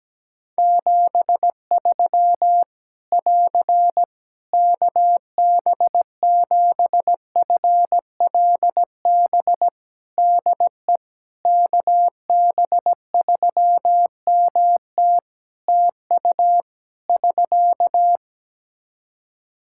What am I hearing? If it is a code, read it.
Morse code, heard as a keyed tone, about 17 wpm, 700 Hz. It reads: 73 <AR> KB7FLB DE KB3MT TU <SK>
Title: Practice QSO - Exchange 4 - Ending - 17wpm